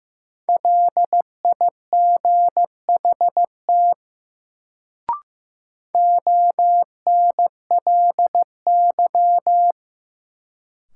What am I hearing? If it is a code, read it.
Morse code, heard as a keyed tone, about 16 wpm, 700 Hz. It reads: LIGHT ONLY
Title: Sentences from Top 200 Words - New Word Drill - 15wpm